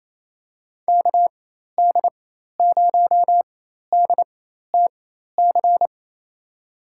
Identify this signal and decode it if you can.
Morse code, heard as a keyed tone, about 27 wpm, spaced as for 13 wpm, 700 Hz. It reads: KD0DTC
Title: USA Call Signs - 6 Characters - 4x Character Spacing - 28wpm